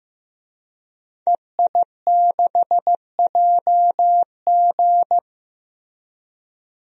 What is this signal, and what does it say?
Morse code, heard as a keyed tone, about 16 wpm, 700 Hz. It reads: EI6JG
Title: DXCC-100 Call Signs - 5 Characters - Repeated 3x - 15wpm